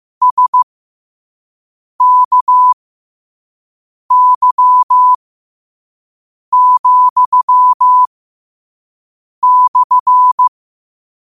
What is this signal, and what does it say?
Morse code, heard as a keyed tone, about 15 wpm, spaced as for 5 wpm, 1000 Hz. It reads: SKY,/
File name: qotdfiles / 5wpm